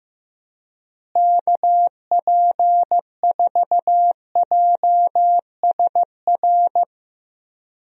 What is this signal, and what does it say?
Morse code, heard as a keyed tone, about 15 wpm, 700 Hz. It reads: KP4JSR
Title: USA Call Signs - 6 Characters - Repeated 2x - 15wpm